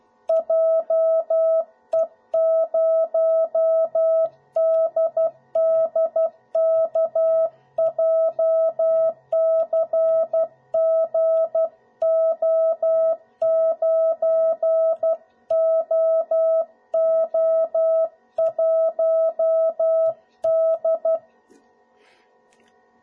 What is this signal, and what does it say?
Morse code, heard as a keyed tone, about 12 wpm, 650 Hz. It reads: JE0DDK JCGO9OO1D
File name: jh8cbh / 2021cw / cw01b